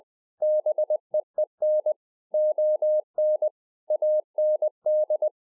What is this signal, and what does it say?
Morse code, heard as a keyed tone, about 20 wpm, 610 Hz. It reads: BEEN ON AND